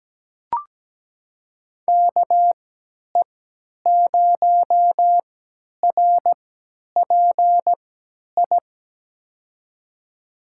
Morse code, heard as a keyed tone, about 17 wpm, spaced as for 10 wpm, 700 Hz. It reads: KE0RPI
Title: USA Call Signs - 3x Character Spacing - 17wpm